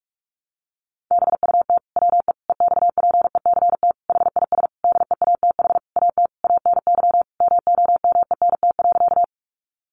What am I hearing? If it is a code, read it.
Morse code, heard as a keyed tone, about 44 wpm, 700 Hz. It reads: BUT WE EXPECT HIS DEATH AT ANY MOMENT.
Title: Sentences from Top 1100 Words 45wpm